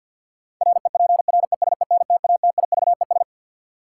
Morse code, heard as a keyed tone, about 52 wpm, 700 Hz. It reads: REPRESENTATIVES
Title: Sets of 5 Words Encore - New Word Drill - 50wpm